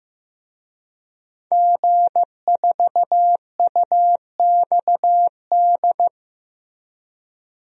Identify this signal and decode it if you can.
Morse code, heard as a keyed tone, about 15 wpm, 700 Hz. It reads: G4UXD
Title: DXCC-100 Call Signs - 5 Characters - Repeated 2x - 15wpm